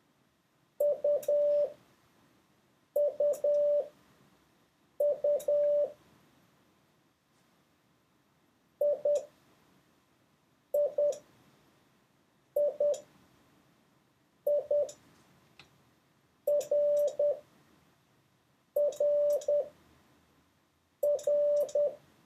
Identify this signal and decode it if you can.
Morse code, heard as a keyed tone, about 10 wpm, spaced as for 5 wpm, 580 Hz. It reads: UUU IIII RRR